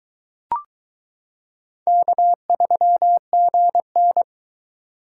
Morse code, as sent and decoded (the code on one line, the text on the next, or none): -.- ...-- --. -.
K3GN